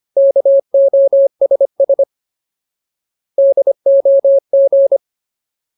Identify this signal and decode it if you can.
Morse code, heard as a keyed tone, about 25 wpm, 550 Hz. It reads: KOSS DOG
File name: Ultimate Word List / 25WPM Ultimate